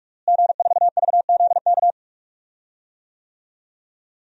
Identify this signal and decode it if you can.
Morse code, heard as a keyed tone, about 46 wpm, 700 Hz. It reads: G4VZK
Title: DXCC-100 Call Signs - 5 Characters - Repeated 2x - 45wpm